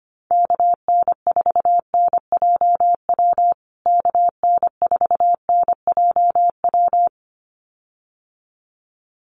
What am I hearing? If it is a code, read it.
Morse code, heard as a keyed tone, about 25 wpm, 700 Hz. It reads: KN4NJW KN4NJW
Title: USA Call Signs Repeated 2x 25wpm